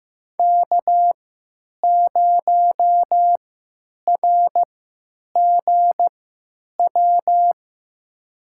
Morse code, heard as a keyed tone, about 15 wpm, spaced as for 9 wpm, 700 Hz. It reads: K0RGW
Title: USA Call Signs - 3x Character Spacing - 15wpm